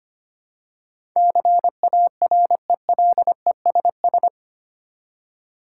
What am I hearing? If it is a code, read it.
Morse code, heard as a keyed tone, about 25 wpm, 700 Hz. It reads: CARELESS
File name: 8 Letter Words - First 3 Letters Given - 25wpm